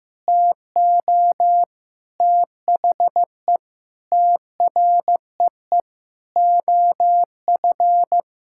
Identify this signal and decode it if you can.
Morse code, heard as a keyed tone, about 15 wpm, 700 Hz. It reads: TO THE TREE OF
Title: Sentences from Top 1000 Words - Review Part 6 - 15wpm